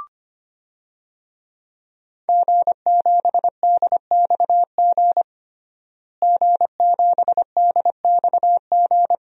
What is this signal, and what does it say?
Morse code, heard as a keyed tone, about 25 wpm, 700 Hz. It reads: G7DXG G7DXG
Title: DXCC-100 Call Signs - 5 Characters - Repeated 3x - 25wpm